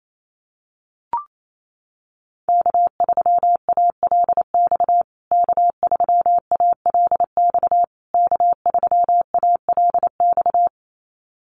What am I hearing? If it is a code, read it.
Morse code, heard as a keyed tone, about 28 wpm, 700 Hz. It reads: K3ALX K3ALX K3ALX
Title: USA Call Signs Repeated 3x 28wpm